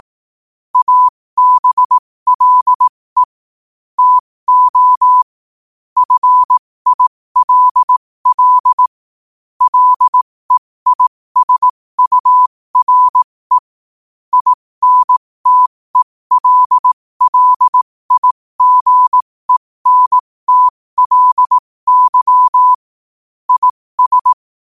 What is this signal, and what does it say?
Morse code, heard as a keyed tone, about 18 wpm, spaced as for 15 wpm, 1000 Hz. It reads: ABLE TO FILL LEISURE INTELLIGENTLY IS
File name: qotdfiles / 15wpm